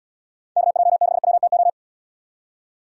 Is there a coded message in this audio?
Yes: Morse code from 0.6 to 1.7 s, heard as a keyed tone, about 68 wpm, 700 Hz. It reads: KJ6WTO